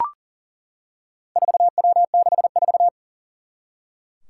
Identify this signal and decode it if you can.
Morse code, heard as a keyed tone, about 38 wpm, 700 Hz. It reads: 4W64
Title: Four Challenging Letter-Number Combinations 40wpm